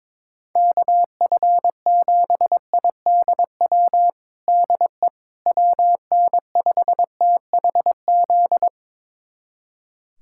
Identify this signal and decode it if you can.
Morse code, heard as a keyed tone, about 22 wpm, 700 Hz. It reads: KF7IDW DE WN5THZ
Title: USA Callsign DE Callsign - 6 Characters - 22wpm